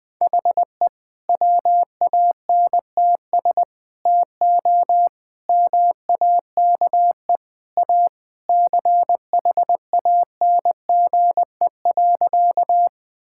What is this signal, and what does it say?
Morse code, heard as a keyed tone, about 20 wpm, 700 Hz. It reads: HE WANTS TO MAKE A CHANGE.
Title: Sentences with 3-Word Phrases - Spoken Beginning - 20wpm